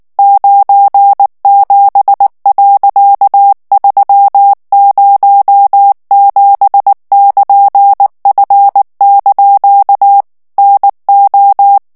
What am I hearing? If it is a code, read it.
Morse code, heard as a keyed tone, about 19 wpm, 800 Hz. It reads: 97.307<KN>F) NO
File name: ANGLAIS texte clair 3 vitesse 18 mots minute